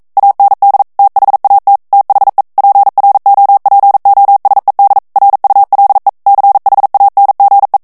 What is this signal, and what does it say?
Morse code, heard as a keyed tone, about 42 wpm, 800 Hz. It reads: AND THAT THE PROPOSED RULE CHANGE